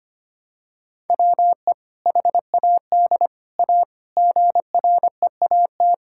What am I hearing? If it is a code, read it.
Morse code, heard as a keyed tone, about 25 wpm, 700 Hz. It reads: WE HAD A GREAT
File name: practice / 25wpm / Sentences with 2-Word Phrases - 25wpm